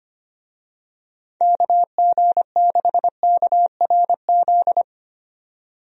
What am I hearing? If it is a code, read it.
Morse code, heard as a keyed tone, about 25 wpm, 700 Hz. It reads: KG6KRZ
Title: USA Call Signs - 6 Characters - 25wpm